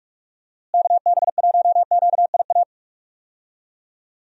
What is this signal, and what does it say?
Morse code, heard as a keyed tone, about 46 wpm, 700 Hz. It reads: KB1QIA
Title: USA Call Signs - 6 Characters - Repeated 3x - 45wpm